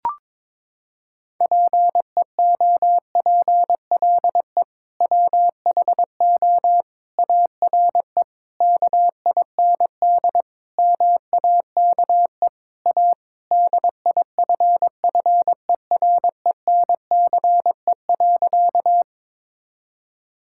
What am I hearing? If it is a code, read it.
Morse code, heard as a keyed tone, about 22 wpm, 700 Hz. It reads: PEOPLE WHO ARE KIND MAKE A DIFFERENCE.
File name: practice / 22wpm / Sentences with 3-Word Phrases - 22wpm